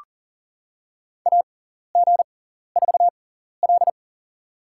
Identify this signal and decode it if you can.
Morse code, heard as a keyed tone, about 40 wpm, spaced as for 14 wpm, 700 Hz. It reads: AG4L
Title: USA Call Signs - 4 Characters - 6x Character Spacing - 40wpm